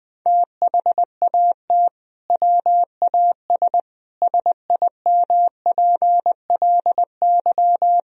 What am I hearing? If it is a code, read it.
Morse code, heard as a keyed tone, about 20 wpm, 700 Hz. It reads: THAT WAS SIMPLY
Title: Sentences from Top 1000 Words - Review Part 4 - 20wpm